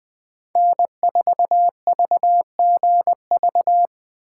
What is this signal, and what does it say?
Morse code, heard as a keyed tone, about 20 wpm, 700 Hz. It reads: N4VGV
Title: USA Call Signs - 5 Characters - 4x Character Spacing - 20wpm